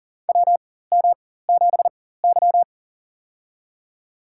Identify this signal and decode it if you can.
Morse code, heard as a keyed tone, about 39 wpm, spaced as for 18 wpm, 690 Hz. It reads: WM7Y
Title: USA Call Signs - 4x Character Spacing - 40wpm